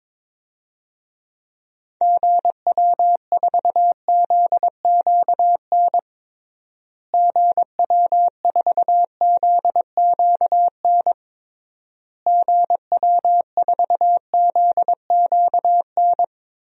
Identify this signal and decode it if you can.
Morse code, heard as a keyed tone, about 22 wpm, 700 Hz. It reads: GW4ZQN GW4ZQN GW4ZQN
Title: DXCC-100 Call Signs - 6 Characters - Repeated 3x - 22wpm